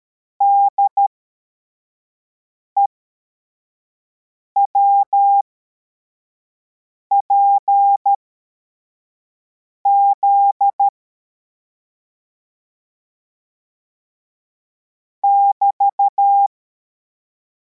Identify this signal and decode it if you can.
Morse code, heard as a keyed tone, about 13 wpm, spaced as for 4 wpm, 800 Hz. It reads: DEWPZ <BT>
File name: formation / cw / Text74